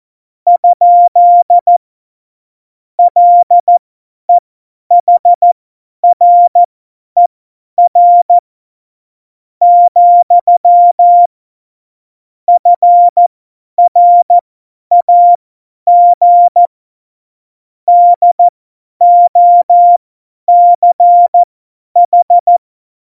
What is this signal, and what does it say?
Morse code, heard as a keyed tone, about 14 wpm, spaced as for 10 wpm, 700 Hz. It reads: ? LEHRER , FRAG DOCH
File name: Morsekurs / Witze / Witz 16. April 2019, 50 F70